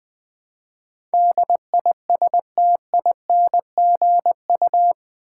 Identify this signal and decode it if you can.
Morse code, heard as a keyed tone, about 20 wpm, 700 Hz. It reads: DISTINGU
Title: Build A Word - 4 Letters At A Time - 11-Letter Words - 20wpm